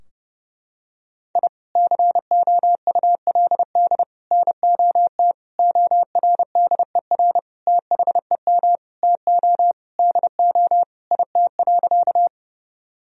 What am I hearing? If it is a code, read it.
Morse code, heard as a keyed tone, about 30 wpm, 700 Hz. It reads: I COULD NOT ORDER THEM TO DO IT.
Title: Sentences from Top 1000 Words - Review Part 2 - 30wpm